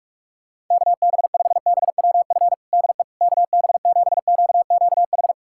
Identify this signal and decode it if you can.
Morse code, heard as a keyed tone, about 42 wpm, 700 Hz. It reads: KB5BWF DE KB7QQH
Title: USA Callsign DE Callsign 45wpm